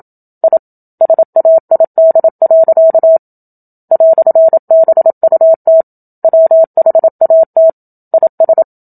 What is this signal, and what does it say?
Morse code, heard as a keyed tone, about 27 wpm, 650 Hz. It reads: I SAID. "BUT WHAT IS
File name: Book Two / 01-Chapter 1 - 27 WPM